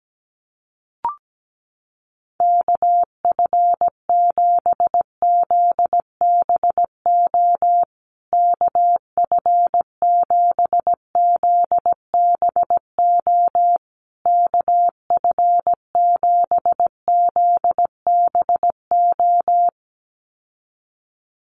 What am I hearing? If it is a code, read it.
Morse code, heard as a keyed tone, about 17 wpm, 700 Hz. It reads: KF7ZBO KF7ZBO KF7ZBO
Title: USA Call Signs Repeated 3x 17wpm